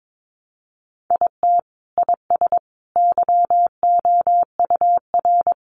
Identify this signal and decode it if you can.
Morse code, heard as a keyed tone, about 22 wpm, 700 Hz. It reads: IT IS YOUR
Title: Sentences from Top 600 Words 22wpm